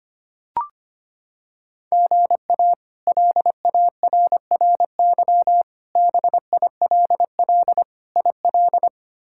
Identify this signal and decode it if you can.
Morse code, heard as a keyed tone, about 25 wpm, 700 Hz. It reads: GA LARRY BILL IL
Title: K1USN SST QSO Practice - 25wpm